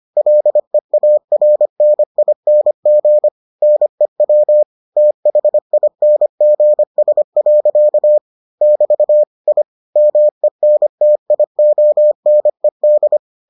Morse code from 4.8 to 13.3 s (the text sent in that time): THINGS. <BT> I MENTIONED